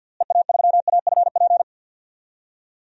Morse code, heard as a keyed tone, about 50 wpm, 700 Hz. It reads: EA3AFP
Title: DXCC-100 Call Signs - 6 Characters - Repeated 2x - 50wpm